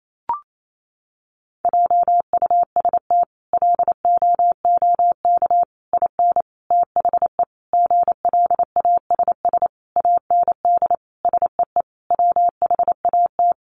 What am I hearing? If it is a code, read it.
Morse code, heard as a keyed tone, about 28 wpm, 700 Hz. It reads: JUST LOOK IN THE GLASS AND SEE WHAT
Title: Sentences from Top 1300 Words 28wpm